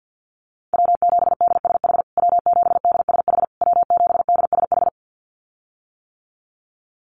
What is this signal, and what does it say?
Morse code, heard as a keyed tone, about 50 wpm, 700 Hz. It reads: W7DSH W7DSH W7DSH
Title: USA Call Signs Repeated 3x 50wpm